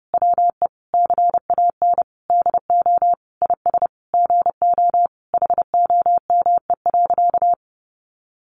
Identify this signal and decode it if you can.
Morse code, heard as a keyed tone, about 28 wpm, 700 Hz. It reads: WE CAN DO IS GO HOME.
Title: Sentences from Top 200 Words 30wpm